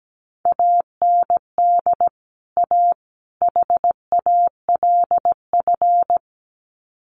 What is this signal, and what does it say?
Morse code, heard as a keyed tone, about 17 wpm, 700 Hz. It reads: AND A HALF
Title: Sets of 3 Words 17wpm